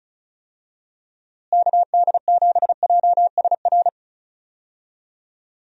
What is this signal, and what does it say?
Morse code, heard as a keyed tone, about 36 wpm, 700 Hz. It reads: KD7JSR
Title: USA Call Signs - 6 Characters - Repeated 3x - 35wpm